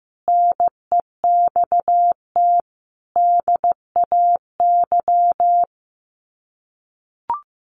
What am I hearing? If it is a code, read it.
Morse code, heard as a keyed tone, about 15 wpm, 700 Hz. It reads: NEXT DAY
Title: Sets of 3 Words 15wpm